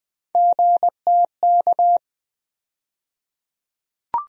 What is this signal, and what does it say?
Morse code, heard as a keyed tone, about 21 wpm, 700 Hz. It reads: GTK
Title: USA Call Sign Postfixes - 3 Letters 20wpm